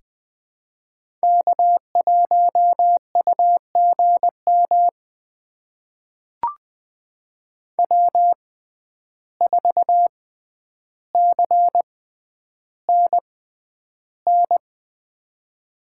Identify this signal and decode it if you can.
Morse code, heard as a keyed tone, about 20 wpm, spaced as for 10 wpm, 700 Hz. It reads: K1UGM W4CNN
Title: USA Call Signs - 5 Characters - 6x Character Spacing - 20wpm